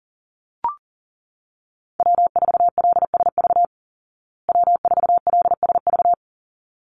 Morse code, heard as a keyed tone, about 40 wpm, 700 Hz. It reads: W4LSV W4LSV
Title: Call Signs Repeated 3x - 40wpm